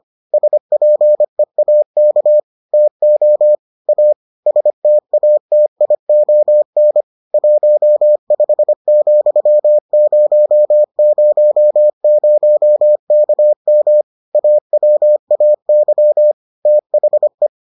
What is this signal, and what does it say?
Morse code, heard as a keyed tone, about 25 wpm, 600 Hz. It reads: SPEAK TO A STATION 15,000KM AWAY THE